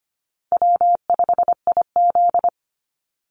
Morse code, heard as a keyed tone, about 25 wpm, 700 Hz. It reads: W5IZ
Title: Easy Call Signs Repeated 3x - 25wpm